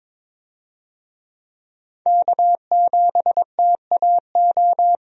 Morse code, heard as a keyed tone, about 22 wpm, 700 Hz. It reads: K7TAO
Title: USA Call Signs - 5 Characters - 22wpm